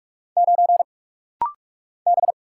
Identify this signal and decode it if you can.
Morse code, heard as a keyed tone, about 47 wpm, 700 Hz. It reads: 9 B
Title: Single Letter-Number - Rapid-Fire 45wpm